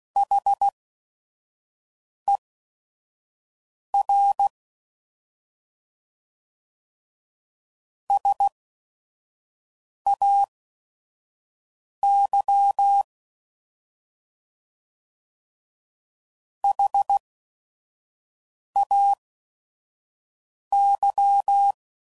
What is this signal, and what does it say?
Morse code, heard as a keyed tone, about 16 wpm, spaced as for 5 wpm, 800 Hz. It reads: HER SAY HAY
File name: Prefixes / By First Letter / S vs Hs